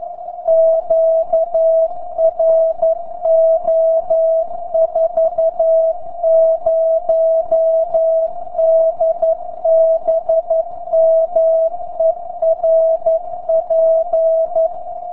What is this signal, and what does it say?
Morse code, heard as a keyed tone, about 11 wpm, 650 Hz. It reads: QRO 40DBM ERP